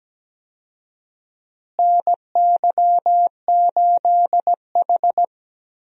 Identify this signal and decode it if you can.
Morse code, heard as a keyed tone, about 17 wpm, 700 Hz. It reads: NY8H
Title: USA Call Signs - 4 Characters - Repeated 3x - 17wpm